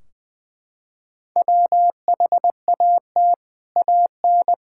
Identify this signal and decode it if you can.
Morse code, heard as a keyed tone, about 20 wpm, 700 Hz. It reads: WHAT AN
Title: Sentences from Top 1600 Words 20wpm